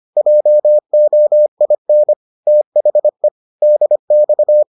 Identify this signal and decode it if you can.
Morse code, heard as a keyed tone, about 25 wpm, 600 Hz. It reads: JOIN THE DX